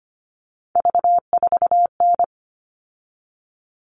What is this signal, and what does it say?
Morse code, heard as a keyed tone, about 25 wpm, 700 Hz. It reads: V4N
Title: All Cut Numbers - 3-Digit 25wpm